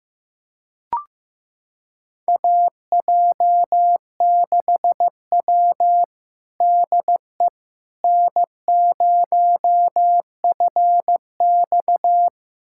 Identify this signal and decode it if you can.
Morse code, heard as a keyed tone, about 15 wpm, 700 Hz. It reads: AJ6W DE N0FX
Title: USA Callsign DE Callsign - 4 Characters - 15wpm